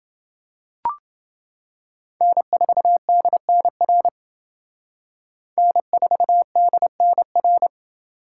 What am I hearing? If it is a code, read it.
Morse code, heard as a keyed tone, about 28 wpm, 700 Hz. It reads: N4DNR N4DNR